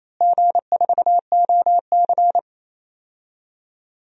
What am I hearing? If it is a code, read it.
Morse code, heard as a keyed tone, about 27 wpm, 700 Hz. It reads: G4OC
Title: DXCC-100 Call Signs - 4 Characters - Repeated 3x - 28wpm